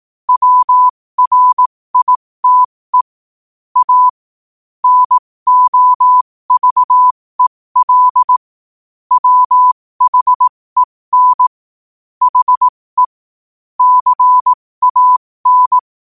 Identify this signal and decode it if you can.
Morse code, heard as a keyed tone, about 18 wpm, spaced as for 15 wpm, 1000 Hz. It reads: WRITE A NOVEL WHEN HE CAN
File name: qotdfiles / 15wpm